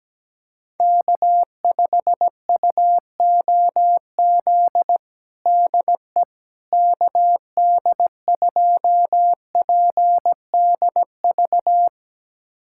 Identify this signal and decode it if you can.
Morse code, heard as a keyed tone, about 17 wpm, 700 Hz. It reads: K5UOZ DE KD2PDV
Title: USA Callsign DE Callsign 17wpm